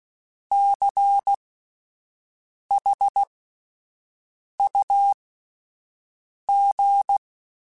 Morse code, heard as a keyed tone, about 16 wpm, spaced as for 5 wpm, 800 Hz. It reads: CHUG
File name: Alphabetical / Alphabetical mp3s / Cs @5